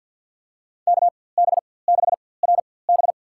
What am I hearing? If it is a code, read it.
Morse code, heard as a keyed tone, about 53 wpm, spaced as for 24 wpm, 700 Hz. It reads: KB6RB
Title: USA Call Signs - 5 Characters - 4x Character Spacing - 50wpm